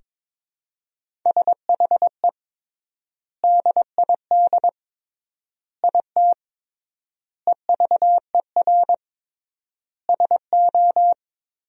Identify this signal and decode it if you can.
Morse code, heard as a keyed tone, about 21 wpm, 700 Hz. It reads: SHE DID IT EVER SO
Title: 3x Word-Spacing - Sentences from Top 400 Words - 22wpm